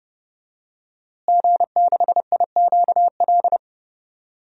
Morse code, heard as a keyed tone, about 30 wpm, 700 Hz. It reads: G6IQL
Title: DXCC-100 Call Signs - 5 Characters - Repeated 2x - 30wpm